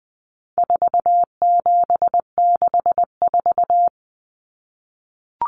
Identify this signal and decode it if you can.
Morse code, heard as a keyed tone, about 19 wpm, 700 Hz. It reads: 4764
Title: Numbers - 4-Digit 20wpm